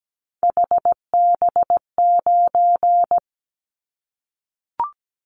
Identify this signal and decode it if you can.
Morse code, heard as a keyed tone, about 17 wpm, 700 Hz. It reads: HB9
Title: Callsign Prefixes for Top 25 DXCC - Repeated 3x - 17wpm